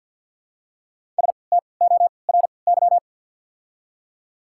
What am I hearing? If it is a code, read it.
Morse code, heard as a keyed tone, about 54 wpm, spaced as for 29 wpm, 700 Hz. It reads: STORY
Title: 3x Character Spacing - Top 500 Words - 50wpm